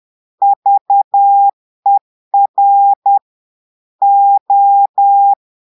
Morse code, heard as a keyed tone, about 10 wpm, 800 Hz. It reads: VER O
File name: portugais texte en clair 2 vitesse 10 mots minutes